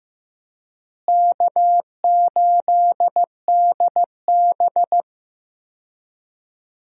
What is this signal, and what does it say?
Morse code, heard as a keyed tone, about 15 wpm, 700 Hz. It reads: K8DB
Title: USA Call Signs - 4 Characters - Repeated 2x - 15wpm